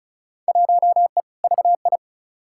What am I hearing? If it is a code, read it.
Morse code, heard as a keyed tone, about 36 wpm, 700 Hz. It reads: 1E VI
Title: Field Day Exchanges - 35wpm